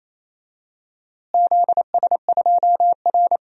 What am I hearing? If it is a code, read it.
Morse code, heard as a keyed tone, about 28 wpm, 700 Hz. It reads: ZS2R
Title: DXCC-100 Call Signs - 4 Characters - Repeated 3x - 28wpm